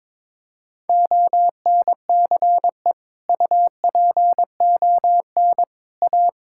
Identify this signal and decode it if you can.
Morse code, heard as a keyed tone, about 22 wpm, 700 Hz. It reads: ONCE UPON A
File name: Sentences with 7-Word Phrases - 22wpm